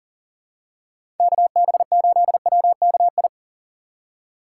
Morse code, heard as a keyed tone, about 40 wpm, 700 Hz. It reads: KB8WKI
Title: USA Call Signs - 6 Characters - 3x Character Spacing - 40wpm